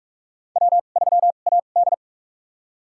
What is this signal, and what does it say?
Morse code, heard as a keyed tone, about 46 wpm, spaced as for 32 wpm, 700 Hz. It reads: W3AD
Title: USA Call Signs - 2x Character Spacing - 45wpm